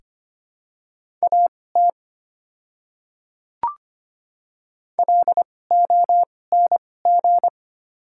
Morse code, heard as a keyed tone, about 25 wpm, spaced as for 18 wpm, 700 Hz. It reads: AT LONG
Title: 2x Character Spacing - Top 200 Words - 25wpm